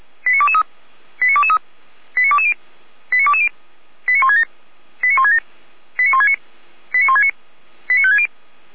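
Five-tone selective call (ZVEI), heard as quick runs of five tones, tones of about 70 ms. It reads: 89223 89223 89220 89220 89167 89167 89169 89169 89560
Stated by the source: ZVEI five-tone selective call 89060 received by SDR.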